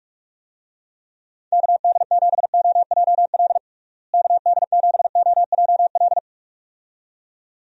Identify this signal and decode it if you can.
Morse code, heard as a keyed tone, about 46 wpm, 700 Hz. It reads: KD7OJL KD7OJL
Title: USA Call Signs - 6 Characters - Repeated 2x - 45wpm